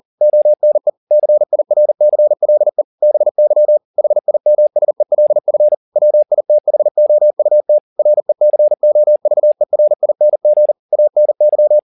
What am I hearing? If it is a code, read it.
Morse code, heard as a keyed tone, about 40 wpm, 600 Hz. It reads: ONE CIRCLE BY HIMSELF WITHOUT RECOVERING ANY
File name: the old man and the sea 2 40wpm